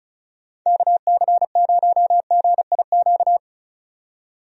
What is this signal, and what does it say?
Morse code, heard as a keyed tone, about 36 wpm, 700 Hz. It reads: KC0GIQ
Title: USA Call Signs - 6 Characters - 35wpm